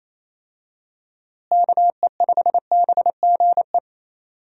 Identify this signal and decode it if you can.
Morse code, heard as a keyed tone, about 28 wpm, 700 Hz. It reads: KE5BGE
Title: USA Call Signs - 6 Characters - Repeated 2x - 28wpm